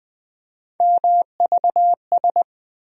Morse code, heard as a keyed tone, about 20 wpm, 700 Hz. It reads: MVS
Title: USA Call Sign Postfixes - 3 Letters 20wpm